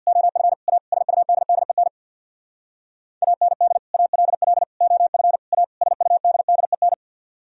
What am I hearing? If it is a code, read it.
Morse code, heard as a keyed tone, about 64 wpm, 700 Hz. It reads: OF A SUDDEN AND ALL OF A SUDDEN